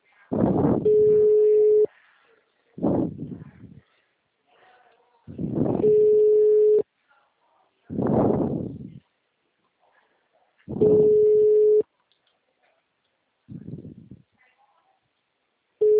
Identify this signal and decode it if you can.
Ringback tone (Europe, 425 Hz).